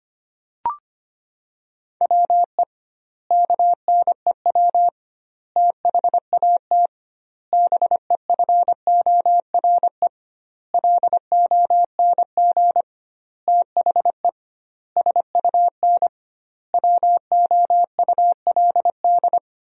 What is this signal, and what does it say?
Morse code, heard as a keyed tone, about 25 wpm, 700 Hz. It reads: WE KNEW THAT BEFORE LONG THE SUN WOULD
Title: 2x Word-Spacing - Sentences from Top 400 Words - 25wpm